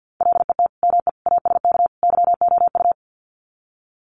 Morse code, heard as a keyed tone, about 50 wpm, 700 Hz. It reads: LET ME ASK YOU
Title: Sets of 4 Words 50wpm